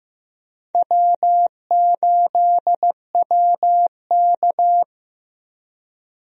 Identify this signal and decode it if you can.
Morse code, heard as a keyed tone, about 15 wpm, 700 Hz. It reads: W8WK
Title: USA Call Signs - 4 Characters - Repeated 2x - 15wpm